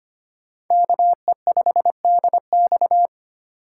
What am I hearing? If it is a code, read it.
Morse code, heard as a keyed tone, about 32 wpm, 700 Hz. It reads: KE5DX